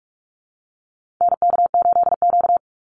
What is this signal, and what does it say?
Morse code, heard as a keyed tone, about 44 wpm, 700 Hz. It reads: NK8Q
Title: NA SOTA Activator Callsigns 45wpm